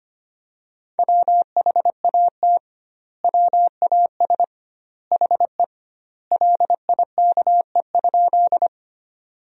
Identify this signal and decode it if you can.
Morse code, heard as a keyed tone, about 25 wpm, 700 Hz. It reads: WHAT WAS HE LIKE?
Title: 2x Word-Spacing - Sentences from Top 100 Words - 25wpm